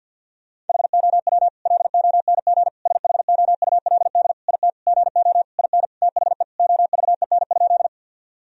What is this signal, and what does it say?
Morse code, heard as a keyed tone, about 51 wpm, 700 Hz. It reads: HOW LONG SHOULD IT GO IN THE OVEN?
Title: Sentences from Top 1000 Words - Review Part 4 - 50wpm